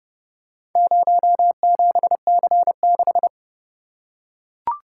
Morse code, heard as a keyed tone, about 28 wpm, 700 Hz. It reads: 07C6